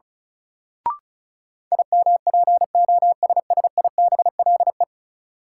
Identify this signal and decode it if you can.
Morse code, heard as a keyed tone, about 34 wpm, 700 Hz. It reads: IMPOSSIBLE
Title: Sentences from Top 1100 Words - New Word Drill - 35wpm